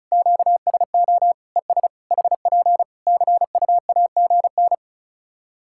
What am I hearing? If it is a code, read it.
Morse code, heard as a keyed tone, about 36 wpm, 700 Hz. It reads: QSO ES HP CUAGN
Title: Practice QSO - Exchange 4 - Ending - 35wpm